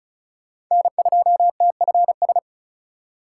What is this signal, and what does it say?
Morse code, heard as a keyed tone, about 36 wpm, 700 Hz. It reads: N2TFS
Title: USA Call Signs - 5 Characters - 3x Character Spacing - 35wpm